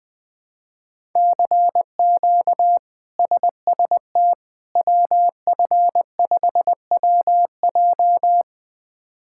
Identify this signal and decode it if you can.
Morse code, heard as a keyed tone, about 20 wpm, 700 Hz. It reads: CQ SST WF5WJ
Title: K1USN SST QSO Practice - 20wpm